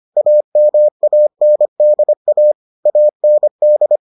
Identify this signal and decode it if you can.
Morse code, heard as a keyed tone, about 25 wpm, 600 Hz. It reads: AMANDA AND